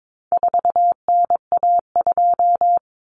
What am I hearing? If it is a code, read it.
Morse code, heard as a keyed tone, about 22 wpm, 700 Hz. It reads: 4NA2
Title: Common Cut Numbers - 4-Digit 22wpm